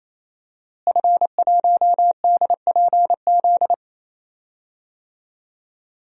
Morse code, heard as a keyed tone, about 27 wpm, 700 Hz. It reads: F1DPZ
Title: DXCC-100 Call Signs - 5 Characters - Repeated 3x - 28wpm